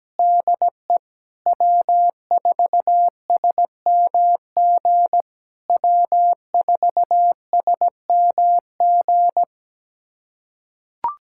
Morse code, heard as a keyed tone, about 17 wpm, 700 Hz. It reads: DE W4SMG W4SMG
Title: SOTA QSO Practice 17wpm